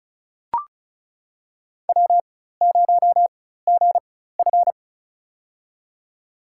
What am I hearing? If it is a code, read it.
Morse code, heard as a keyed tone, about 36 wpm, spaced as for 16 wpm, 700 Hz. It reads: W0GF
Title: USA Call Signs - 4 Characters - 4x Character Spacing - 35wpm